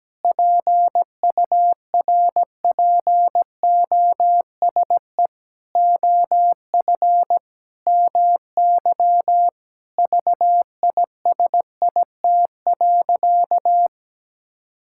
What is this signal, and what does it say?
Morse code, heard as a keyed tone, about 17 wpm, 700 Hz. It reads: PURPOSE OF MY VISIT.
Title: Sentences from Top 1000 Words - Review Part 6 - 17wpm